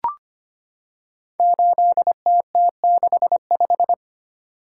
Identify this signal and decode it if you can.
Morse code, heard as a keyed tone, about 25 wpm, 700 Hz. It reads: 8TT65
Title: Code Groups - 5 Letter:Numbers - 25wpm